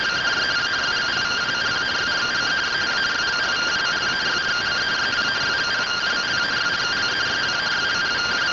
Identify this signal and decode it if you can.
Two-tone teleprinter signal, about 170 Hz shift.